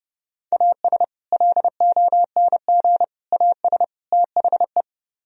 Morse code, heard as a keyed tone, about 31 wpm, 700 Hz. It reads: AS LONG AS THE